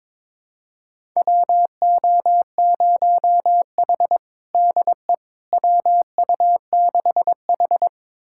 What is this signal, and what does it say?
Morse code, heard as a keyed tone, about 22 wpm, 700 Hz. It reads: WO0H DE WU6H
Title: USA Callsign DE Callsign - 4 Characters - 22wpm